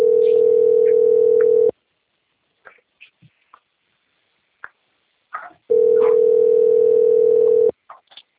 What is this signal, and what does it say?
Ringback tone (North America), 440 + 480 Hz.
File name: out- - - - - .20824